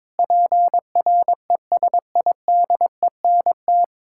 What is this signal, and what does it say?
Morse code, heard as a keyed tone, about 22 wpm, 700 Hz. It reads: PRESIDENT
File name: Sentences with 7-Word Phrases - 22wpm